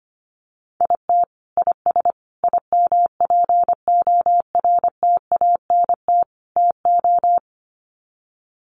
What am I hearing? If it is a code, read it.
Morse code, heard as a keyed tone, about 25 wpm, 700 Hz. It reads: IT IS IMPORTANT TO
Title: Sets of 4 Words 25wpm